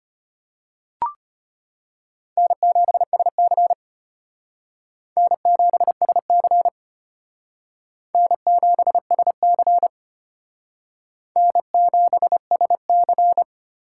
Morse code, heard as a keyed tone, about 32 wpm, 700 Hz. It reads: N7SC N7SC N7SC N7SC